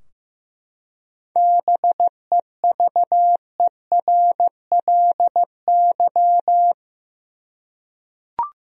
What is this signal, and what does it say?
Morse code, heard as a keyed tone, about 15 wpm, 700 Hz. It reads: BEVERLY
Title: USA Common Names - 15wpm